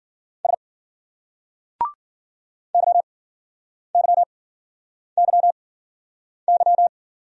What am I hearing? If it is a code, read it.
Morse code, heard as a keyed tone, about 48 wpm, spaced as for 9 wpm, 700 Hz. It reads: S YYYY
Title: Speed-Racing Single Letters 40wpm